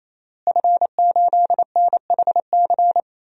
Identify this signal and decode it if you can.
Morse code, heard as a keyed tone, about 28 wpm, 700 Hz. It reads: F8NHC
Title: DXCC-100 Call Signs - 5 Characters - Repeated 3x - 28wpm